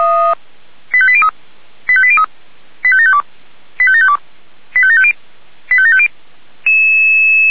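Five-tone selective call (ZVEI), heard as quick runs of five tones, tones of about 70 ms. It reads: 85803 85803 85752 85752 85750 85750
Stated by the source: ZVEI five-tone selective call received via SDR.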